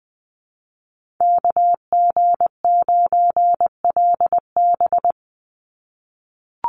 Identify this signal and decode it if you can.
Morse code, heard as a keyed tone, about 20 wpm, 700 Hz. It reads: KG9LB
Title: Call Signs - 20wpm